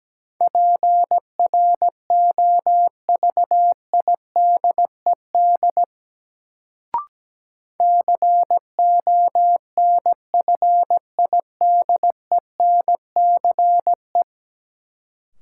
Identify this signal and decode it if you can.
Morse code, heard as a keyed tone, about 17 wpm, 700 Hz. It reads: PROVIDED CONFIDENCE
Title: Sentences from Top 1500 Words - New Word Drill Alt - 17wpm